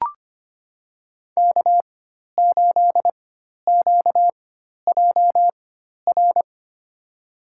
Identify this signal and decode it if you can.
Morse code, heard as a keyed tone, about 25 wpm, spaced as for 12 wpm, 700 Hz. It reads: K8QJR
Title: USA Call Signs - 5 Characters - 4x Character Spacing - 25wpm